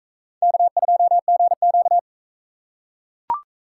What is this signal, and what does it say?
Morse code, heard as a keyed tone, about 41 wpm, 700 Hz. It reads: K2GQ